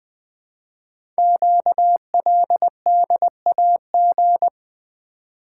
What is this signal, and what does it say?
Morse code, heard as a keyed tone, about 20 wpm, 700 Hz. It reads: QLDAG
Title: Code Groups - 5 Letters - 20wpm